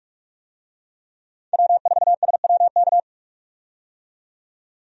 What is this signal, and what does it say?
Morse code, heard as a keyed tone, about 46 wpm, 700 Hz. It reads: W4SWK
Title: USA Call Signs - 5 Characters - 45wpm